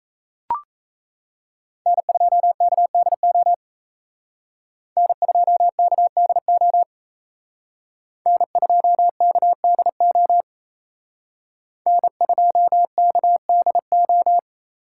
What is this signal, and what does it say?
Morse code, heard as a keyed tone, about 34 wpm, 700 Hz. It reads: N2KDO N2KDO N2KDO N2KDO
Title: Speed Racing - USA 1x3 Call Signs 28wpm